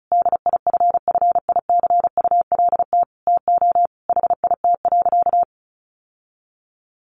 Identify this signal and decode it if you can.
Morse code, heard as a keyed tone, about 34 wpm, 700 Hz. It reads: DIFFICULT TO HIT.
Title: Sentences from Top 1200 Words 35wpm